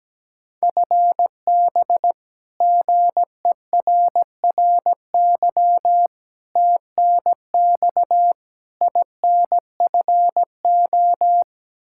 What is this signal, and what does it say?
Morse code, heard as a keyed tone, about 17 wpm, 700 Hz. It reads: FB GERRY TNX INFO
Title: Practice QSO - Exchange 3 - More Info - 17wpm